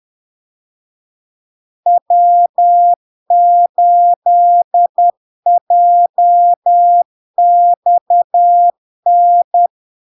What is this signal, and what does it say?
Morse code, heard as a keyed tone, about 10 wpm, 700 Hz. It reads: W8JXN
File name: LARK-FD-2022-Log-in-CW-10wpm-700-Hz